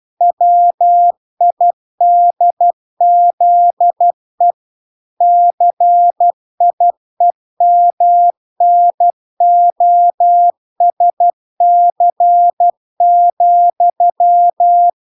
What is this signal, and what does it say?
Morse code, heard as a keyed tone, about 12 wpm, 700 Hz. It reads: WIDZE CIEMNOSC,